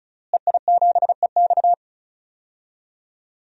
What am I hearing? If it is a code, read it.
Morse code, heard as a keyed tone, about 36 wpm, 700 Hz. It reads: EI7EX
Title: DXCC-100 Call Signs - 5 Characters - Repeated 3x - 35wpm